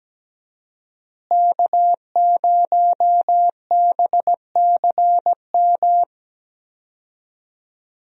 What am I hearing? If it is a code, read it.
Morse code, heard as a keyed tone, about 17 wpm, 700 Hz. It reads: K0BCM
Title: USA Call Signs - 5 Characters - Repeated 2x - 17wpm